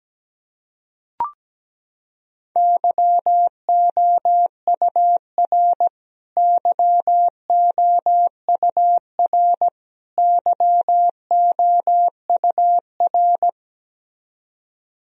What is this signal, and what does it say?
Morse code, heard as a keyed tone, about 17 wpm, 700 Hz. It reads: YOUR YOUR YOUR
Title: Top 100 Words Repeated 17wpm